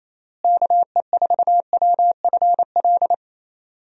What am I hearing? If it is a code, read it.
Morse code, heard as a keyed tone, about 28 wpm, 700 Hz. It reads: KE4WFL
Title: USA Call Signs - 6 Characters - Repeated 3x - 28wpm